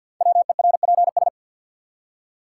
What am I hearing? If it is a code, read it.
Morse code, heard as a keyed tone, about 52 wpm, 700 Hz. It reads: WERPS
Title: Code Groups - 5 Letters - 50wpm